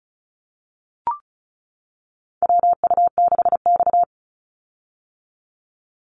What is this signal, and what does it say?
Morse code, heard as a keyed tone, about 34 wpm, 700 Hz. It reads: WU6X
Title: Call Signs - 35wpm